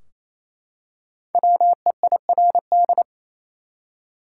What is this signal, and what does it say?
Morse code, heard as a keyed tone, about 28 wpm, 700 Hz. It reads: WEIRD
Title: Five Letter Words - 28wpm